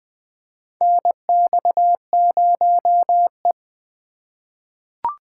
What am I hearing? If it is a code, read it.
Morse code, heard as a keyed tone, about 20 wpm, 700 Hz. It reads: NX0E
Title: USA Call Signs - 4 Characters - 4x Character Spacing - 20wpm